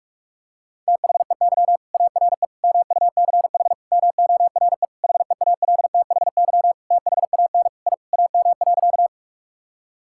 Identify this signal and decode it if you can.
Morse code, heard as a keyed tone, about 46 wpm, 700 Hz. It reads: THEY ARE MUCH MORE HEALTHY THAN I AM.
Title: Sentences from Top 1000 Words - Review Part 5 - 45wpm